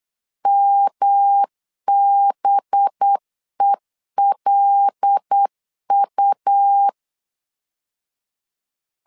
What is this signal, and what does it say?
Morse code, heard as a keyed tone, about 8 wpm, 800 Hz. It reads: MBELU
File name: LEZIONI / 6 PKX0 / 6 P 8wpm 3025